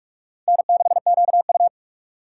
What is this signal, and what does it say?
Morse code, heard as a keyed tone, about 46 wpm, 690 Hz. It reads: N6QU
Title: USA Call Signs - 4 Characters - 3x Character Spacing - 45wpm